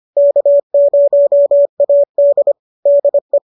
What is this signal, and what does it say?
Morse code, heard as a keyed tone, about 25 wpm, 590 Hz. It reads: K0AD DE
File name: qso213 25